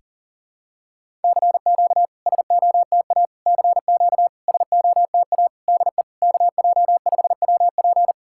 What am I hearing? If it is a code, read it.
Morse code, heard as a keyed tone, about 40 wpm, 700 Hz. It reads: CQ SOTA CQ SOTA DE KJ5WP
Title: SOTA QSO Practice 40wpm